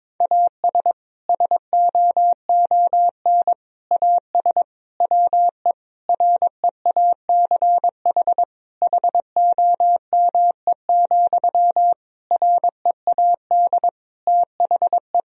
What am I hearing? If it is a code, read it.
Morse code, heard as a keyed tone, about 22 wpm, 700 Hz. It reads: AS SOON AS WE REACH HOME, READ THE
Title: Sentences from Top 1000 Words - Review Part 2 - 22wpm